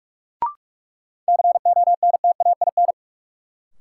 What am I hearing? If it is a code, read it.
Morse code, heard as a keyed tone, about 44 wpm, 700 Hz. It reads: CONTAIN
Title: Sentences from Top 500 Words - New Word Drill - 45wpm